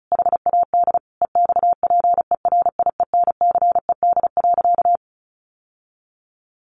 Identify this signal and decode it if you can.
Morse code, heard as a keyed tone, about 34 wpm, 700 Hz. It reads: HAD EXPERIENCED.
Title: Sentences from Top 1100 Words 35wpm